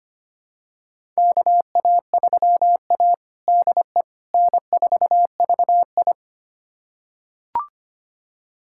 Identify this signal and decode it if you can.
Morse code, heard as a keyed tone, about 25 wpm, 700 Hz. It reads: KA3A DE N4VI
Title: USA Callsign DE Callsign - 4 Characters - 25wpm